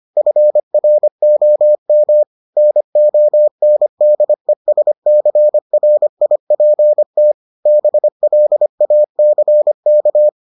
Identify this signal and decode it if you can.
Morse code, heard as a keyed tone, about 25 wpm, 600 Hz. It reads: FROM NONDESCRIPT BLACK